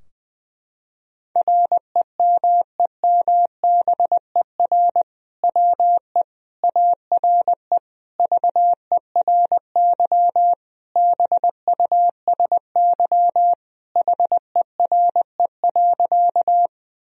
Morse code, heard as a keyed tone, about 20 wpm, 700 Hz. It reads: REMEMBER WE ARE VERY BUSY HERE.
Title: Sentences from Top 1000 Words - Review Part 3 - 20wpm